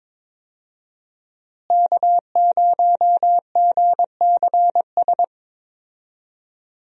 Morse code, heard as a keyed tone, about 22 wpm, 700 Hz. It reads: K0GCS
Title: USA Call Signs - 5 Characters - Repeated 2x - 22wpm